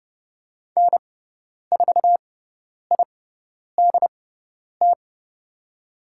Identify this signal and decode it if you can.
Morse code, heard as a keyed tone, about 30 wpm, spaced as for 10 wpm, 700 Hz. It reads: N4IDT
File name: Lesson 16g - Letter c - Call Signs 30wpm